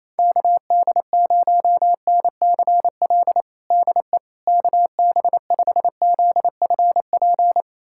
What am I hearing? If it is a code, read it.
Morse code, heard as a keyed tone, about 28 wpm, 700 Hz. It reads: KD0NCL DE KB5ZFP
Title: USA Callsign DE Callsign - 6 Characters - 28wpm